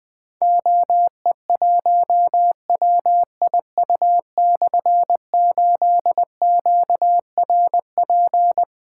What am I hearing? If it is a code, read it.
Morse code, heard as a keyed tone, about 20 wpm, 700 Hz. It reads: OE1WIU/8QRP
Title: Difficult Call Signs 20wpm